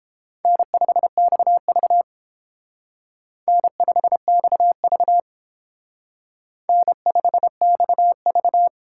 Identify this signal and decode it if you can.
Morse code, heard as a keyed tone, about 29 wpm, 700 Hz. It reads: N5XV N5XV N5XV